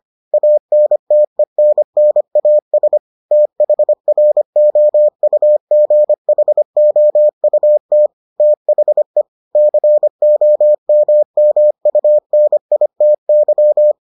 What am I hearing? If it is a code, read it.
Morse code, heard as a keyed tone, about 25 wpm, 600 Hz. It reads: ANTENNAS THROUGHOUT THE COMMUNITY